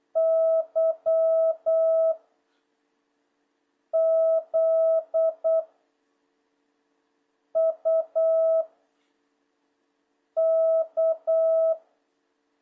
Morse code, heard as a keyed tone, about 8 wpm, 650 Hz. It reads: Y Z U K
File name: jh8cbh / 2026cw